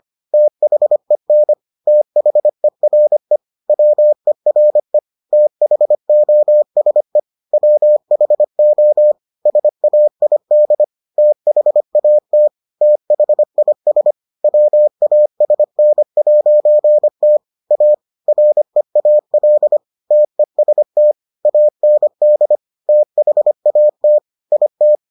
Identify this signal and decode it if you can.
Morse code, heard as a keyed tone, about 25 wpm, 600 Hz. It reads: THEN THERE WERE THOSE WHO SAID THAT THIS WASN'T A REAL TEST AND THAT IT